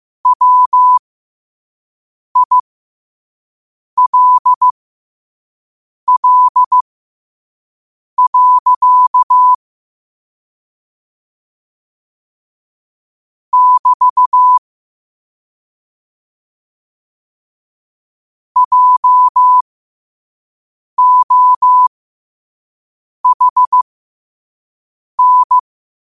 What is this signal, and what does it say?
Morse code, heard as a keyed tone, about 15 wpm, spaced as for 5 wpm, 1000 Hz. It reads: WILL. <BT> JOHN
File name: qotdfiles / 5wpm